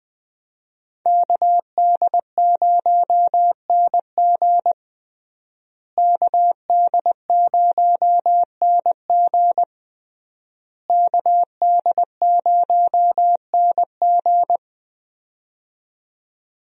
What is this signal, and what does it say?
Morse code, heard as a keyed tone, about 20 wpm, 700 Hz. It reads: KD0NG KD0NG KD0NG
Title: USA Call Signs - 5 Characters - Repeated 3x - 20wpm